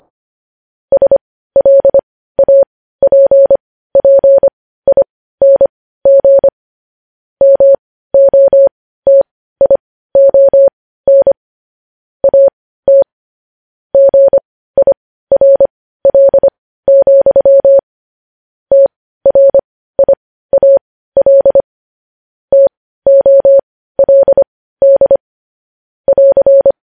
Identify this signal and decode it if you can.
Morse code, heard as a keyed tone, about 25 wpm, spaced as for 15 wpm, 550 Hz. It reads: SLAPPING MOTION AT GIRL, TRIAL TOLD <AR>
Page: News Headlines in Morse Code at 15 WPM